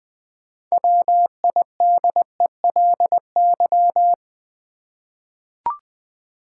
Morse code, heard as a keyed tone, about 20 wpm, 700 Hz. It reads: WIDELY
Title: Six Letter Words - 20wpm